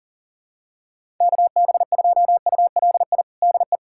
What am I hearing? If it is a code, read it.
Morse code, heard as a keyed tone, about 40 wpm, 700 Hz. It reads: KB2ULI DE
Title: USA Callsign DE Callsign 40wpm